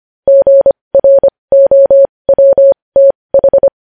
Morse code, heard as a keyed tone, about 25 wpm, spaced as for 20 wpm, 550 Hz. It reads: GROWTH
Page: News Headlines in Morse Code at 20 WPM